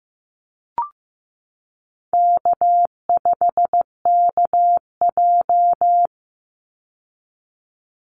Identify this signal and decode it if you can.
Morse code, heard as a keyed tone, about 15 wpm, 700 Hz. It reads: K5KJ
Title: Call Signs Repeated 3x - 15wpm